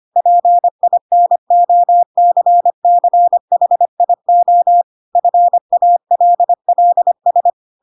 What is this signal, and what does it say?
Morse code, heard as a keyed tone, about 25 wpm, 700 Hz. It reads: PINOCCHIO FALLS
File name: Pinocchio / Pinocchio ch6 25 25 ik1hge